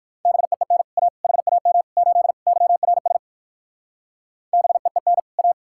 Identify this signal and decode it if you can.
Morse code, heard as a keyed tone, about 58 wpm, 700 Hz. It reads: BEEN A HAM 8 YRS BEEN A